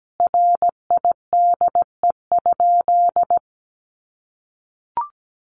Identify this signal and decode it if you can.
Morse code, heard as a keyed tone, about 17 wpm, 700 Hz. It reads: RIDE?
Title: Sentences from Top 1200 Words 17wpm